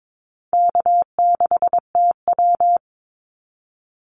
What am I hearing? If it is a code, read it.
Morse code, heard as a keyed tone, about 22 wpm, 700 Hz. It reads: K6TW
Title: NA SOTA Activator Callsigns 22wpm